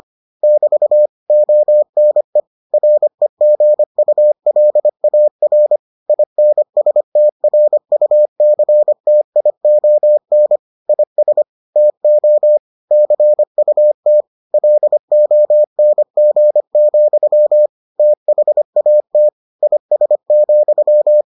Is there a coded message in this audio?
Yes: Morse code from 0.4 to 21.2 s, heard as a keyed tone, about 25 wpm, 600 Hz. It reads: <BT> ONE REGULAR INSTRUCTION IS TO CUT LONG, THAT IS,